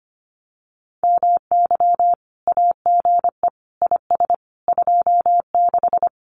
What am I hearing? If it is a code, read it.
Morse code, heard as a keyed tone, about 25 wpm, 700 Hz. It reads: MY AGE IS 26
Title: Common QSO Phrases Encore 25wpm